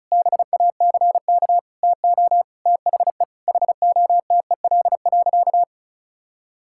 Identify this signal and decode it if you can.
Morse code, heard as a keyed tone, about 36 wpm, 700 Hz. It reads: BACK TO THE HOTEL.
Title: Sentences from Top 1600 Words 35wpm